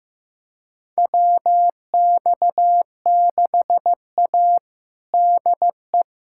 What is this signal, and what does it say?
Morse code, heard as a keyed tone, about 16 wpm, 700 Hz. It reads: WX6A DE
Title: USA Callsign DE Callsign - 4 Characters - 15wpm